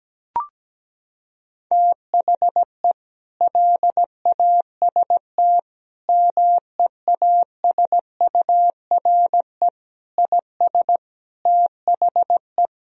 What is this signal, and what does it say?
Morse code, heard as a keyed tone, about 17 wpm, 700 Hz. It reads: THE LAST MEASURE IS THE
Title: Sentences from Top 1000 Words - Review Part 2 - 17wpm